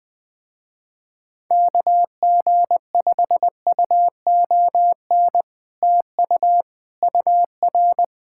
Morse code, heard as a keyed tone, about 20 wpm, 700 Hz. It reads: KG5UON TU UR
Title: Speed-Racing Common QSO Phrases - 17wpm